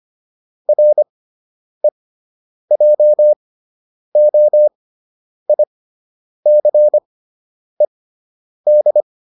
Morse code, heard as a keyed tone, about 25 wpm, spaced as for 9 wpm, 600 Hz. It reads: REJOICED